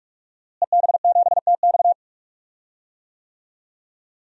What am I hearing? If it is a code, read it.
Morse code, heard as a keyed tone, about 46 wpm, 700 Hz. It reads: EB7TX
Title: DXCC-100 Call Signs - 5 Characters - Repeated 2x - 45wpm